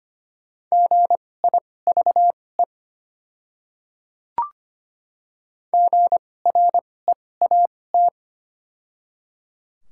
Morse code, heard as a keyed tone, about 25 wpm, spaced as for 18 wpm, 700 Hz. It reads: GIVE GREAT
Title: 2x Character Spacing - Top 500 Words - 25wpm